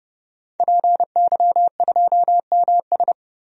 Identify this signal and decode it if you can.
Morse code, heard as a keyed tone, about 32 wpm, 700 Hz. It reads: PY2MS
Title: DXCC-100 Call Signs - 5 Characters - 30wpm